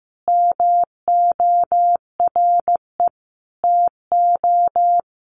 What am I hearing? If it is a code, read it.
Morse code, heard as a keyed tone, about 15 wpm, 700 Hz. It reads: MORE TO
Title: Sentences from Top 100 Words 15wpm